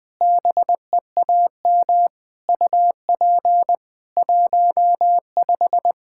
Morse code, heard as a keyed tone, about 20 wpm, 700 Hz. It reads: BEAM UP 15
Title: Practice QSO - Exchange 2 - Info - 20wpm